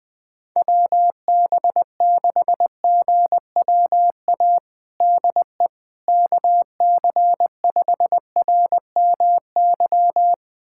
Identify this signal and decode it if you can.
Morse code, heard as a keyed tone, about 20 wpm, 700 Hz. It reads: WB6GWA DE KC5RMY
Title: USA Callsign DE Callsign 20wpm